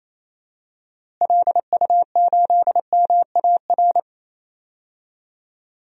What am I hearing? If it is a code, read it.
Morse code, heard as a keyed tone, about 28 wpm, 700 Hz. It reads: LU8MAR
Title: DXCC-100 Call Signs - 6 Characters - Repeated 2x - 28wpm